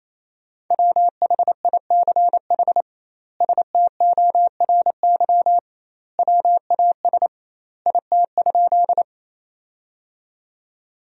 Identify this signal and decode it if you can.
Morse code, heard as a keyed tone, about 28 wpm, 700 Hz. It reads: WHICH STORY WAS IT?
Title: 2x Word-Spacing - Sentences from Top 300 Words - 28wpm